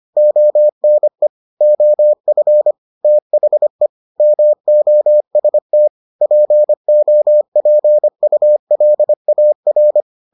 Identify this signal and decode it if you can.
Morse code, heard as a keyed tone, about 25 wpm, 600 Hz. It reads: ONE OF THE MOST POPULAR